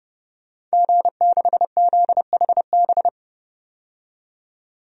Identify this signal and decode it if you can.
Morse code, heard as a keyed tone, about 32 wpm, 700 Hz. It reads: G6ZHB
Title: DXCC-100 Call Signs - 5 Characters - 30wpm